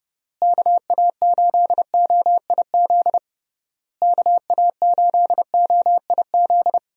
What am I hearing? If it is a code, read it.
Morse code, heard as a keyed tone, about 29 wpm, 700 Hz. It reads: KA8OIZ KA8OIZ
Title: USA Call Signs - 6 Characters - Repeated 3x - 30wpm